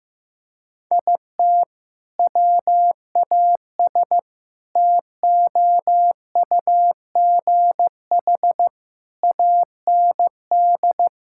Morse code, heard as a keyed tone, about 15 wpm, 700 Hz. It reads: IT WAS TOUGH AND
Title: Sentences from Top 1600 Words 15wpm